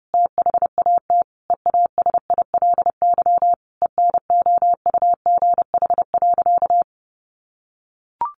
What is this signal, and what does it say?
Morse code, heard as a keyed tone, about 30 wpm, 700 Hz. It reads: THAT EASILY ENOUGH.
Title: Sentences from Top 900 Words 30wpm